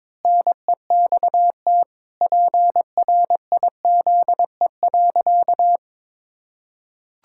Morse code, heard as a keyed tone, about 21 wpm, 700 Hz. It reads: NEXT PRIZE.
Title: Pangram Challenge v2 - 22wpm